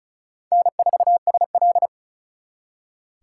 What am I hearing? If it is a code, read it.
Morse code, heard as a keyed tone, about 42 wpm, 700 Hz. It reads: N4SL
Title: Speed Racing - USA 1x2 Call Signs 35wpm